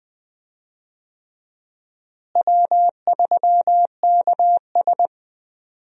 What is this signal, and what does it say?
Morse code, heard as a keyed tone, about 20 wpm, 700 Hz. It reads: W3KS
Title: USA Call Signs - 4 Characters - Repeated 3x - 20wpm